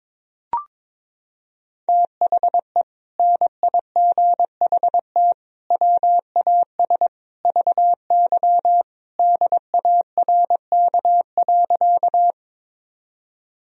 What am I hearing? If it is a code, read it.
Morse code, heard as a keyed tone, about 22 wpm, 700 Hz. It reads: THE NIGHT WAS VY DARK.
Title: Sentences from Top 500 Words and Abbreviations 22wpm